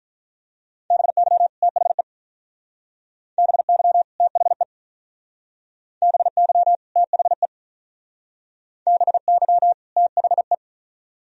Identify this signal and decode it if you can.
Morse code, heard as a keyed tone, about 43 wpm, 700 Hz. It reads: BY THE BY THE BY THE BY THE
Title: Speed-Racing - Sets of 2 Words 35wpm